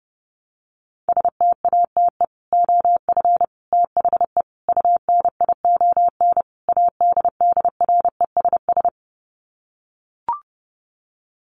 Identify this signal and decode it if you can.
Morse code, heard as a keyed tone, about 30 wpm, 700 Hz. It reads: STATE OF THE UNION ADDRESS
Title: Sets of 5 Words Encore 30wpm